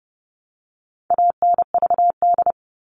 Morse code, heard as a keyed tone, about 30 wpm, 700 Hz. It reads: ANVD
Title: All Cut Numbers - 4-Digit 30wpm